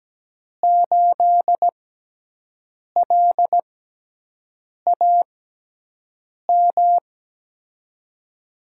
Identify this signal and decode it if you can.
Morse code, heard as a keyed tone, about 17 wpm, 700 Hz. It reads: 8 L A M
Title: USA Call Signs - 6 Characters - 6x Character Spacing - 17wpm